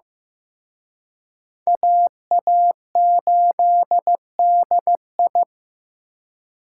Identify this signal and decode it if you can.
Morse code, heard as a keyed tone, about 18 wpm, 700 Hz. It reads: AA8DI
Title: Speed Racing - USA 2x2 Call Signs 15wpm